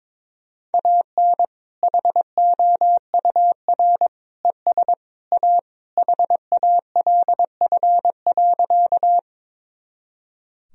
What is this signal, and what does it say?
Morse code, heard as a keyed tone, about 22 wpm, 700 Hz. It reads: AN HOUR ES A HALF.
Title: Sentences from Top 400 Words and Abbreviations 22wpm